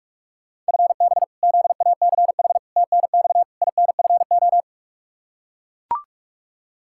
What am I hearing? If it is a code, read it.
Morse code, heard as a keyed tone, about 41 wpm, 700 Hz. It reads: FB ZACH TNX INFO
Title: K1USN SST QSO Practice 45wpm